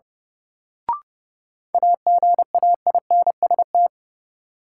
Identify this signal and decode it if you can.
Morse code, heard as a keyed tone, about 30 wpm, 700 Hz. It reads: AGAINST
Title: Sets of 2 Words - New Word Drill - 30wpm